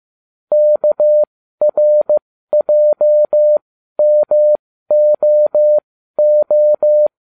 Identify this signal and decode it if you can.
Morse code, heard as a keyed tone, about 15 wpm, spaced as for 12 wpm, 600 Hz. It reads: KRJMOO